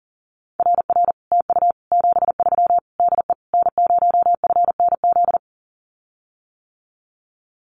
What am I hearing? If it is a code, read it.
Morse code, heard as a keyed tone, about 40 wpm, 700 Hz. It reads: RR TU 73 DE N0FNZ
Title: Common QSO Phrases Encore 40wpm